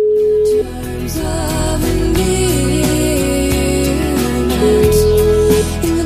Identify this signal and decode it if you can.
Ringback tone (Europe, 425 Hz), with noise.